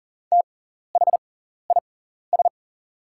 Morse code, heard as a keyed tone, about 40 wpm, spaced as for 14 wpm, 690 Hz. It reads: THIS
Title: 6x Character Spacing - Top 300 Words - 40wpm